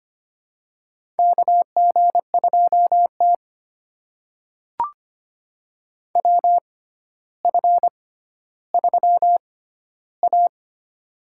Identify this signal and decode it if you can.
Morse code, heard as a keyed tone, about 25 wpm, spaced as for 13 wpm, 700 Hz. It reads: KG2T WF3A
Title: USA Call Signs - 4 Characters - 6x Character Spacing - 25wpm